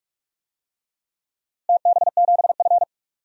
Morse code, heard as a keyed tone, about 42 wpm, 700 Hz. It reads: TB7F
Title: DXCC-100 Call Signs - 4 Characters - Repeated 2x - 45wpm